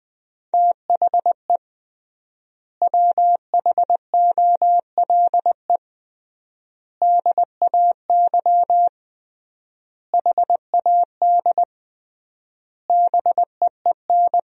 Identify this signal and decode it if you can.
Morse code, heard as a keyed tone, about 20 wpm, 700 Hz. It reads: THE WHOLE DAY HAD BEEN
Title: 3x Word-Spacing - Sentences from Top 500 Words - 20wpm